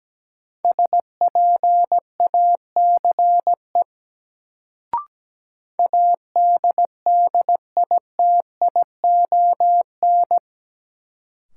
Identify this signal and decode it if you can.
Morse code, heard as a keyed tone, about 17 wpm, 700 Hz. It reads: SPACE ADDITION
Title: Sentences from Top 1500 Words - New Word Drill Alt - 17wpm